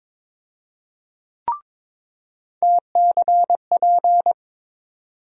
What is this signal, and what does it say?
Morse code, heard as a keyed tone, about 22 wpm, 700 Hz. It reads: TCP
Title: USA Call Sign Postfixes - 3 Letters 22wpm